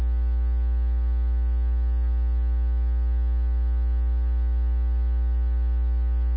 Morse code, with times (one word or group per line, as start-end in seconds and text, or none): none